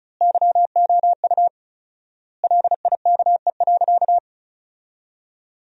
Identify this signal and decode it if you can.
Morse code, heard as a keyed tone, about 36 wpm, 700 Hz. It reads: YOU LIKE.
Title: 4x Word-Spacing - Sentences from Top 500 Words - 35wpm